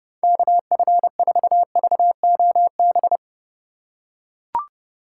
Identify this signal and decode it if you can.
Morse code, heard as a keyed tone, about 32 wpm, 700 Hz. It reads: KF4VOB
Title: USA Call Signs - 6 Characters - 3x Character Spacing - 30wpm